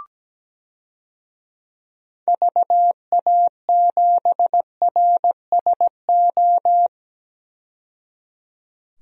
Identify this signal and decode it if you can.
Morse code, heard as a keyed tone, about 17 wpm, 700 Hz. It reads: VA7RSO
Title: DXCC-100 Call Signs - 6 Characters - 17wpm